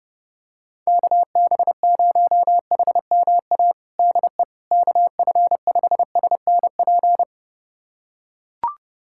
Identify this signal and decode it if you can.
Morse code, heard as a keyed tone, about 32 wpm, 700 Hz. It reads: KB0HMA DE KF5SNP
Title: USA Callsign DE Callsign - 6 Characters - 30wpm